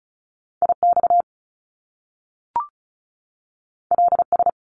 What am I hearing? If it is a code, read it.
Morse code, heard as a keyed tone, about 36 wpm, 700 Hz. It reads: IX LS
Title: Top Two Letter Combinations in Words 35wpm